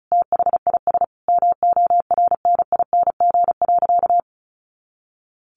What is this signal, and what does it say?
Morse code, heard as a keyed tone, about 34 wpm, 700 Hz. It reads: THIS MORNING.
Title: Sentences from Top 400 Words 35wpm